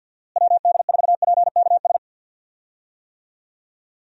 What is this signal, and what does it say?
Morse code, heard as a keyed tone, about 52 wpm, 700 Hz. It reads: WD4PKS
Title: USA Call Signs - 6 Characters - Repeated 3x - 50wpm